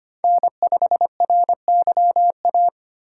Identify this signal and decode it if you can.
Morse code, heard as a keyed tone, about 25 wpm, 700 Hz. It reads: N5RYA
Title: USA Call Signs - 5 Characters - 2x Character Spacing - 25wpm